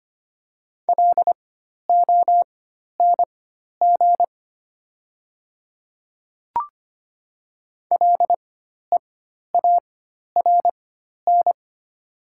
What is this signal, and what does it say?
Morse code, heard as a keyed tone, about 25 wpm, spaced as for 12 wpm, 700 Hz. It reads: LONG LEARN
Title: 4x Character Spacing - Top 300 Words - 25wpm